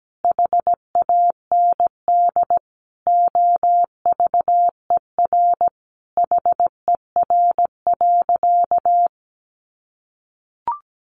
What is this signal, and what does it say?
Morse code, heard as a keyed tone, about 17 wpm, 700 Hz. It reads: HAND OVER HER.
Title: Sentences from Top 1400 Words 17wpm